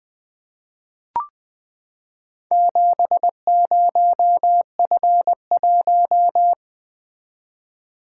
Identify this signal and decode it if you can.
Morse code, heard as a keyed tone, about 20 wpm, 700 Hz. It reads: 70F1
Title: Four Letter-Number Combinations - 20wpm